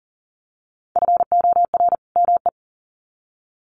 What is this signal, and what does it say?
Morse code, heard as a keyed tone, about 37 wpm, 700 Hz. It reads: FOR ME
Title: Sets of 2 Words 40wpm